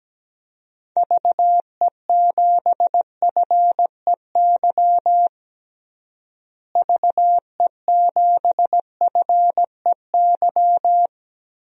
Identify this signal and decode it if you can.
Morse code, heard as a keyed tone, about 17 wpm, 700 Hz. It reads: VE7FEY VE7FEY
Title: DXCC-100 Call Signs - 6 Characters - Repeated 3x - 17wpm